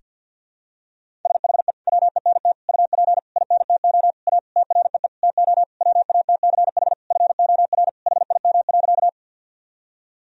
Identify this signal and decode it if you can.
Morse code, heard as a keyed tone, about 52 wpm, 700 Hz. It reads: SHE WENT UP INTO A TREE TO WATCH FOR HIM.
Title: Sentences from Top 1000 Words - Review Part 1 - 50wpm